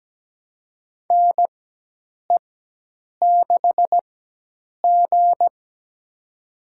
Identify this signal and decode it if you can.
Morse code, heard as a keyed tone, about 17 wpm, spaced as for 8 wpm, 700 Hz. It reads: NE6G
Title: USA Call Signs - 4 Characters - 4x Character Spacing - 17wpm